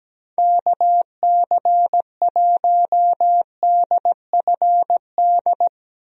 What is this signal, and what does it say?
Morse code, heard as a keyed tone, about 17 wpm, 700 Hz. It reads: KC1DFD
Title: USA Call Signs - 6 Characters - Repeated 2x - 17wpm